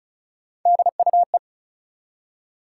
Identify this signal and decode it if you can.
Morse code, heard as a keyed tone, about 36 wpm, 700 Hz. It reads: DUE
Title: Three Letter Words - 35wpm